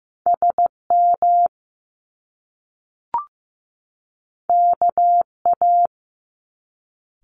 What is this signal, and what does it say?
Morse code, heard as a keyed tone, about 15 wpm, 700 Hz. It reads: SM KA
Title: Top Two Letter Combinations in Words 15wpm